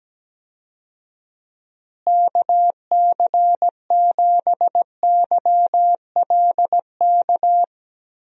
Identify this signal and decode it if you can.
Morse code, heard as a keyed tone, about 17 wpm, 700 Hz. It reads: KC7YLK
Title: USA Call Signs - 6 Characters - Repeated 2x - 17wpm